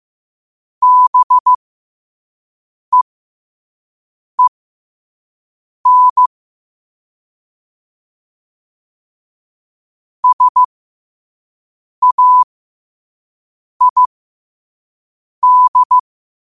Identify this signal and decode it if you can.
Morse code, heard as a keyed tone, about 15 wpm, spaced as for 5 wpm, 1000 Hz. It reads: BEEN SAID
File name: qotdfiles / 5wpm